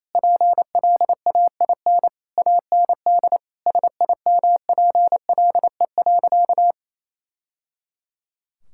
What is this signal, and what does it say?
Morse code, heard as a keyed tone, about 28 wpm, 700 Hz. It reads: PLAIN AND SIMPLE.
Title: Sentences from Top 1000 Words - Review Part 5 - 28wpm